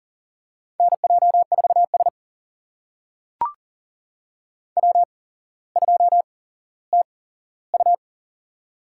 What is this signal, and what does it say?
Morse code, heard as a keyed tone, about 40 wpm, spaced as for 17 wpm, 700 Hz. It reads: NJ4S W2TU
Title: USA Call Signs - 8x Character Spacing - 40wpm